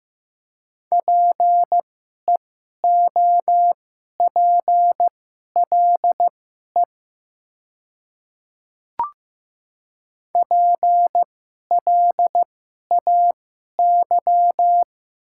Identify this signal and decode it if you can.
Morse code, heard as a keyed tone, about 15 wpm, spaced as for 11 wpm, 700 Hz. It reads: PEOPLE PLAY
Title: 2x Character Spacing - Top 500 Words - 15wpm